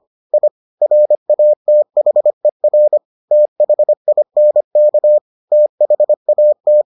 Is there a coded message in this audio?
Yes: Morse code from 0.3 to 6.8 s, heard as a keyed tone, about 25 wpm, 600 Hz. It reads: I RATHER THINK THAT